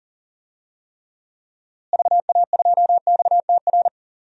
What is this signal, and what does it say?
Morse code, heard as a keyed tone, about 40 wpm, 700 Hz. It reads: VA2XTR
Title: DXCC-100 Call Signs - 6 Characters - Repeated 3x - 40wpm